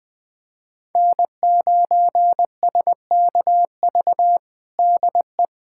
Morse code, heard as a keyed tone, about 20 wpm, 700 Hz. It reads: N9SKV DE
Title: USA Callsign DE Callsign - 5 Characters - 20wpm